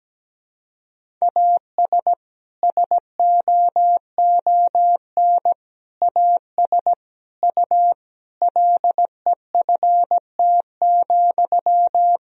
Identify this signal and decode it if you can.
Morse code, heard as a keyed tone, about 17 wpm, 700 Hz. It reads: AS SOON AS U LEFT,
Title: Sentences from Top 400 Words and Abbreviations 17wpm